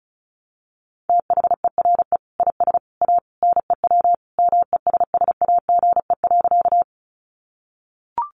Morse code, heard as a keyed tone, about 34 wpm, 700 Hz. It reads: THERE IS A NEW MESSAGE.
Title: Sentences from Top 1100 Words 35wpm